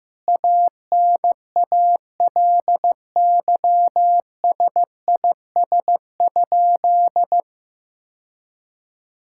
Morse code, heard as a keyed tone, about 15 wpm, 700 Hz. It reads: ANALYSIS?
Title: Sentences from Top 1000 Words - Review Part 5 - 15wpm